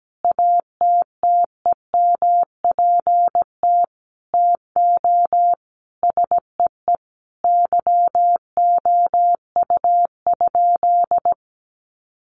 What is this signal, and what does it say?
Morse code, heard as a keyed tone, about 17 wpm, 700 Hz. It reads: ATTEMPT TO SEE YOU?
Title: Sentences from Top 1000 Words 17wpm